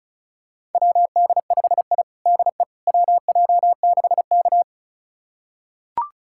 Morse code, heard as a keyed tone, about 36 wpm, 700 Hz. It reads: WD5I DE WJ6K
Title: USA Callsign DE Callsign - 4 Characters - 35wpm